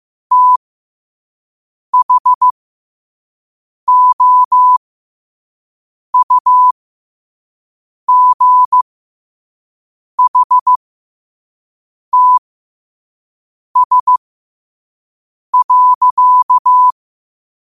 Morse code, heard as a keyed tone, about 15 wpm, spaced as for 5 wpm, 1000 Hz. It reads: THOUGHTS.
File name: qotdfiles / 5wpm